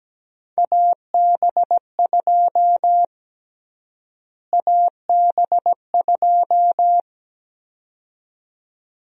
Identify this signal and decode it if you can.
Morse code, heard as a keyed tone, about 17 wpm, 700 Hz. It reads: AB2 AB2
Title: USA Call Sign Prefixes - Repeated 2x - 17wpm